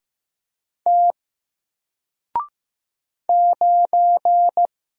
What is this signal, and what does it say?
Morse code, heard as a keyed tone, about 15 wpm, 700 Hz. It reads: T 9
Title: Single Letter-Number Alt 15wpm